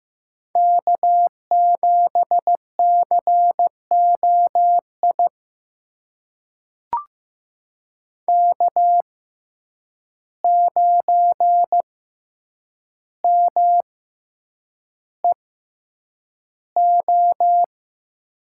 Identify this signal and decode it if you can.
Morse code, heard as a keyed tone, about 15 wpm, spaced as for 8 wpm, 700 Hz. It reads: K7COI K9MEO
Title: USA Call Signs - 5 Characters - 6x Character Spacing - 15wpm